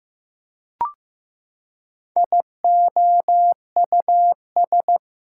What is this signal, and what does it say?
Morse code, heard as a keyed tone, about 15 wpm, 700 Hz. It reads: IOUS
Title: Prefixes and Suffixes 15wpm